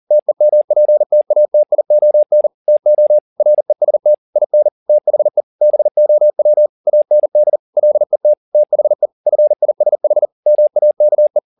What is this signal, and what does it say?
Morse code, heard as a keyed tone, about 40 wpm, 600 Hz. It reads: TEMPTATION TO REST IN THE BOW AND LET THE FISH MAKE
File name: the old man and the sea 2 40wpm